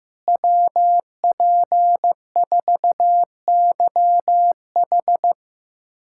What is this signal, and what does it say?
Morse code, heard as a keyed tone, about 15 wpm, 700 Hz. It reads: WP4YH
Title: USA Callsign DE Callsign - 5 Characters - 15wpm